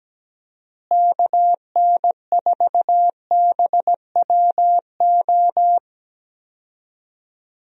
Morse code, heard as a keyed tone, about 17 wpm, 700 Hz. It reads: KN4BWO
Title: USA Call Signs - 6 Characters - Repeated 2x - 17wpm